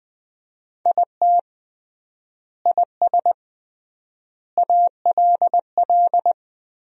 Morse code, heard as a keyed tone, about 21 wpm, 700 Hz. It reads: IT IS ALL
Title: 3x Word-Spacing - Sentences from Top 200 Words - 20wpm